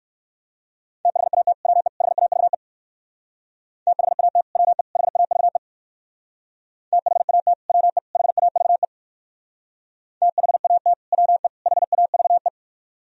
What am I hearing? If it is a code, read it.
Morse code, heard as a keyed tone, about 57 wpm, 700 Hz. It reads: THAT WE HAVE THAT WE HAVE THAT WE HAVE THAT WE HAVE